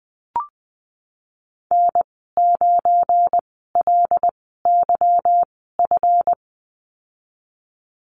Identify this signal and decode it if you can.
Morse code, heard as a keyed tone, about 20 wpm, spaced as for 15 wpm, 700 Hz. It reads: N9LYF
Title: 186 Lesson 30g - Letter b - Call Signs 20wpm